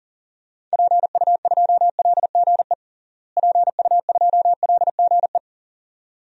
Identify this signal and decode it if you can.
Morse code, heard as a keyed tone, about 40 wpm, 700 Hz. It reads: PU2LGE PU2LGE
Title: DXCC-100 Call Signs - 6 Characters - Repeated 2x - 40wpm